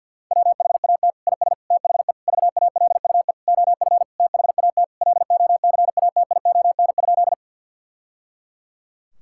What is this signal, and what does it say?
Morse code, heard as a keyed tone, about 50 wpm, 700 Hz. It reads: WHAT IS THE VALUE OF THAT LOCATION?
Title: Sentences from Top 1000 Words - Review Part 6 - 50wpm